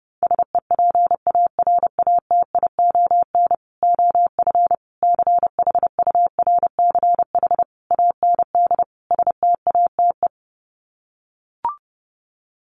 Morse code, heard as a keyed tone, about 30 wpm, 700 Hz. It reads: SEPARATION OF CHURCH AND STATE
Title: Sets of 5 Words 30wpm